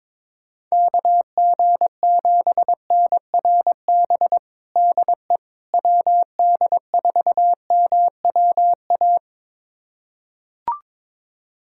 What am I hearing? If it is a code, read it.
Morse code, heard as a keyed tone, about 22 wpm, 700 Hz. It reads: KG7NRB DE WD4MWA
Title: USA Callsign DE Callsign 22wpm